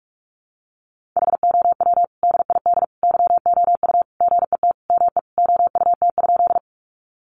Morse code, heard as a keyed tone, about 44 wpm, 700 Hz. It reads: HOW DID YOU GET ME OUT?
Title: Sentences from Top 200 Words 45wpm